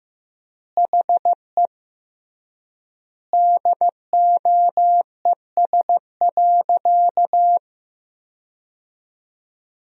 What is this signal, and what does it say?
Morse code, heard as a keyed tone, about 16 wpm, 700 Hz. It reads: HE DOES.
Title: 3x Word-Spacing - Sentences from Top 500 Words - 15wpm